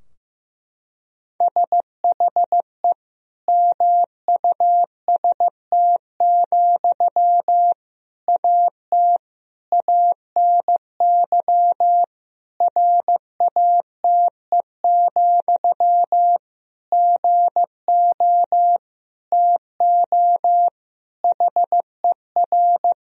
Morse code, heard as a keyed tone, about 15 wpm, 700 Hz. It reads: SHE MUST, AT ANY RATE, GO TO HER
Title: Sentences from Top 1000 Words - Review Part 4 - 15wpm